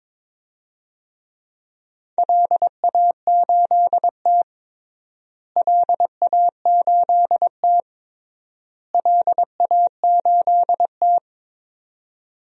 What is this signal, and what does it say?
Morse code, heard as a keyed tone, about 22 wpm, 700 Hz. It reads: LA8T LA8T LA8T
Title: DXCC-100 Call Signs - 4 Characters - Repeated 3x - 22wpm